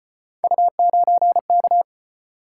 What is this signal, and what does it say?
Morse code, heard as a keyed tone, about 33 wpm, 700 Hz. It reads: U9K